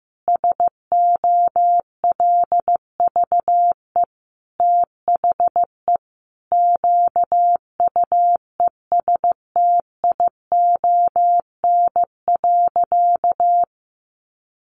Morse code, heard as a keyed tone, about 15 wpm, 700 Hz. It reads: SOLVE THE QUESTION.
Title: Sentences from Top 1000 Words 15wpm